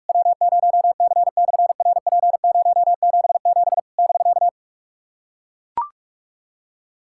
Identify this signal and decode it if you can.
Morse code, heard as a keyed tone, about 44 wpm, 700 Hz. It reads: W0C/RP077 <BK>
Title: SOTA QSO Practice 45wpm